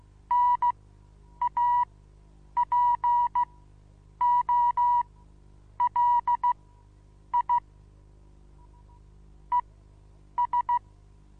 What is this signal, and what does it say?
Morse code, heard as a keyed tone, about 16 wpm, spaced as for 8 wpm, 970 Hz. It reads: NAPOLI ES
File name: Onlinelehrgang / morsen / DJ4UF Morsen25